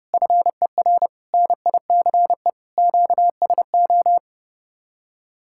Morse code, heard as a keyed tone, about 36 wpm, 700 Hz. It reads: FER NICE QSO
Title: Speed-Racing Common QSO Phrases - 30wpm